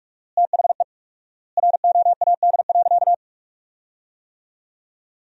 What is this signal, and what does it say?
Morse code, heard as a keyed tone, about 43 wpm, 700 Hz. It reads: THE ROAD.
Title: 4x Word-Spacing - Sentences from Top 400 Words - 45wpm